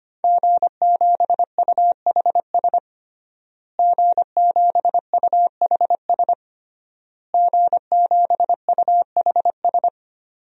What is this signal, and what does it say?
Morse code, heard as a keyed tone, about 25 wpm, 700 Hz. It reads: G7UHS G7UHS G7UHS
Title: DXCC-100 Call Signs - 5 Characters - Repeated 3x - 25wpm